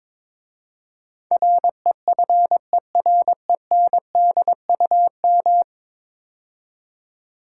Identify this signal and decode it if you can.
Morse code, heard as a keyed tone, about 22 wpm, 700 Hz. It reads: REFERENDUM
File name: practice / 22wpm / 10 Letter Words - First 3 Letters Given - 22wpm